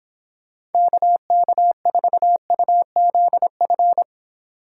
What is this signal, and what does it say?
Morse code, heard as a keyed tone, about 29 wpm, 700 Hz. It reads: KK4UZF